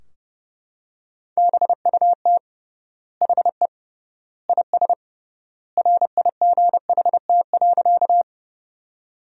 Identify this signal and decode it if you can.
Morse code, heard as a keyed tone, about 29 wpm, 700 Hz. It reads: BUT HE IS RIGHT.
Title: 3x Word-Spacing - Sentences from Top 200 Words - 30wpm